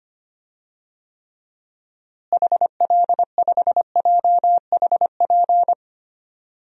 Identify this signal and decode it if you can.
Morse code, heard as a keyed tone, about 25 wpm, 700 Hz. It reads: HL5JHP
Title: DXCC-100 Call Signs - 6 Characters - Repeated 2x - 25wpm